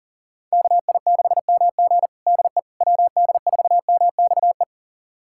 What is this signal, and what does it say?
Morse code, heard as a keyed tone, about 37 wpm, 700 Hz. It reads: KI6MG DE WD4MXE
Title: USA Callsign DE Callsign 40wpm